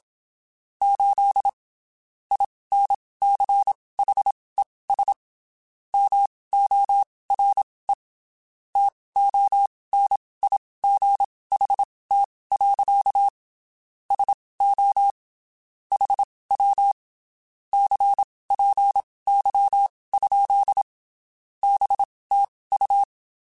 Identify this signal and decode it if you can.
Morse code, heard as a keyed tone, about 27 wpm, spaced as for 19 wpm, 800 Hz. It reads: 8 INCHES MORE TONIGHT. SO HW CPY? BTU
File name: (spaced) / QSOs Spaced-mp3s / WA7QOX 1 @25 sp